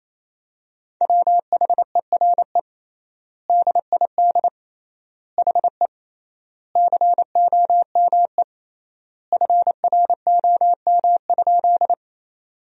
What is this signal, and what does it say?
Morse code, heard as a keyed tone, about 28 wpm, 700 Hz. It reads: WHERE DID HE COME FROM?
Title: 3x Word-Spacing - Sentences from Top 200 Words - 28wpm